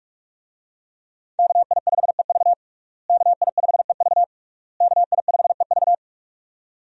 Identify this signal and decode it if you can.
Morse code, heard as a keyed tone, about 41 wpm, 700 Hz. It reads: KI5EV KI5EV KI5EV
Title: USA Call Signs - 5 Characters - Repeated 3x - 45wpm